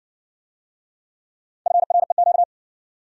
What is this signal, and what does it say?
Morse code, heard as a keyed tone, about 58 wpm, 700 Hz. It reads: FREQ